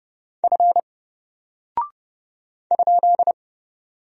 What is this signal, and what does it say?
Morse code, heard as a keyed tone, about 30 wpm, 700 Hz. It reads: F ?
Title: Single Character (Letter:Number:Punctuation:Prosign) 30wpm